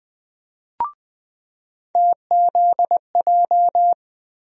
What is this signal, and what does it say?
Morse code, heard as a keyed tone, about 20 wpm, 700 Hz. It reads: TZJ
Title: USA Call Sign Postfixes - 3 Letters 20wpm